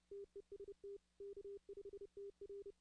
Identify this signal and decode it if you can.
Morse code, heard as a keyed tone, about 29 wpm, 400 Hz. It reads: TEST K5TR
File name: 2016-ARRL-DX-CW